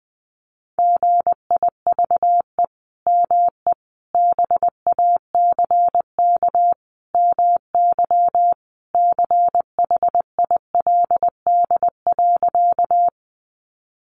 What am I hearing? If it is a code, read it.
Morse code, heard as a keyed tone, about 20 wpm, 700 Hz. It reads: GIVE ME BACK MY CHILD.
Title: Sentences from Top 800 Words 20wpm